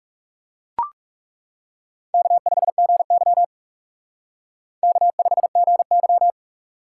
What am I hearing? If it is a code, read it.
Morse code, heard as a keyed tone, about 36 wpm, 700 Hz. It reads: K5GY K5GY
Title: Speed Racing - USA 1x2 Call Signs 30wpm